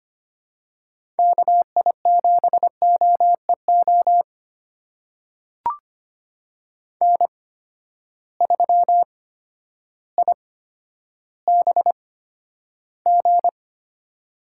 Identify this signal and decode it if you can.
Morse code, heard as a keyed tone, about 25 wpm, spaced as for 11 wpm, 700 Hz. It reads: KI7OEO N3IBG
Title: USA Call Signs - 8x Character Spacing - 25wpm